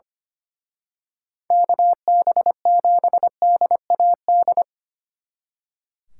Morse code, heard as a keyed tone, about 29 wpm, 700 Hz. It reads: KB7DAD
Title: Speed Racing - USA 2x3 Call Signs 25wpm